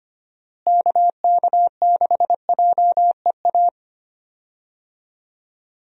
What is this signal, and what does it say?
Morse code, heard as a keyed tone, about 25 wpm, 700 Hz. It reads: KK6JEA
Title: USA Call Signs - 6 Characters - Repeated 2x - 25wpm